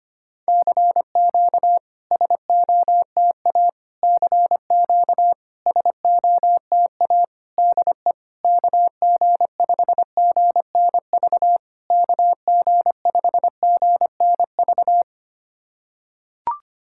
Morse code, heard as a keyed tone, about 25 wpm, 700 Hz. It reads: CQ SOTA CQ SOTA DE KG5GNV KG5GNV
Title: SOTA QSO Practice 25wpm